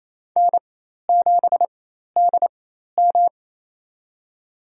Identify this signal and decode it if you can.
Morse code, heard as a keyed tone, about 27 wpm, spaced as for 13 wpm, 700 Hz. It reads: N7DM
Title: USA Call Signs - 4 Characters - 4x Character Spacing - 28wpm